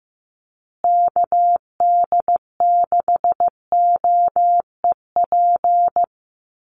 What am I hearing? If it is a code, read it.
Morse code, heard as a keyed tone, about 15 wpm, 700 Hz. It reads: KD6OEP
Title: USA Call Signs Repeated 2x 15wpm